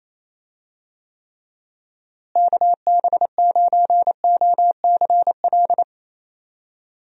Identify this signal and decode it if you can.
Morse code, heard as a keyed tone, about 28 wpm, 700 Hz. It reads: KB9OCL
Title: USA Call Signs - 6 Characters - Repeated 3x - 28wpm